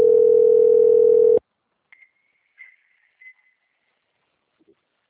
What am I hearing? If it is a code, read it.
Ringback tone (North America), 440 + 480 Hz.